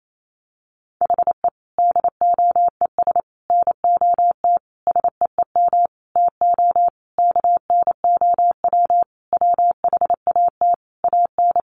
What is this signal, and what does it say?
Morse code, heard as a keyed tone, about 28 wpm, 700 Hz. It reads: HE DOES NOT SEEM TO KNOW WHAT AN
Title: Sentences from Top 1400 Words 28wpm